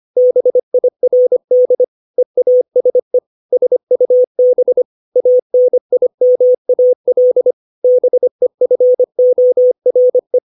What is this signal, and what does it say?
Morse code, heard as a keyed tone, about 25 wpm, 500 Hz. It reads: BIRD EASE SUB ANIMAL BEFORE